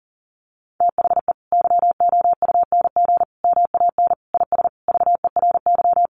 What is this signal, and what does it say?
Morse code, heard as a keyed tone, about 40 wpm, 700 Hz. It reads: THE YOUNG MAN IS VERY
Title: Sentences from Top 1200 Words 40wpm